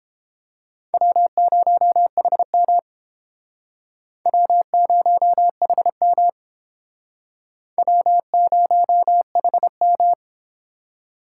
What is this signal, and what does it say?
Morse code, heard as a keyed tone, about 32 wpm, 700 Hz. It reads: W0HM W0HM W0HM